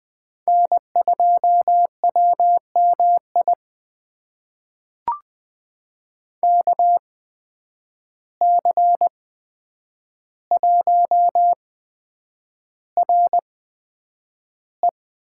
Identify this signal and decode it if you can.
Morse code, heard as a keyed tone, about 20 wpm, spaced as for 9 wpm, 700 Hz. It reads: N2WMI KC1RE
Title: USA Call Signs - 5 Characters - 8x Character Spacing - 20wpm